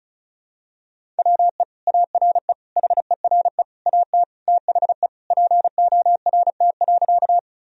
Morse code, heard as a keyed tone, about 34 wpm, 700 Hz. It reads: WE ARE HERE AT THE PORT.
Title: Sentences from Top 1000 Words - Review Part 1 - 35wpm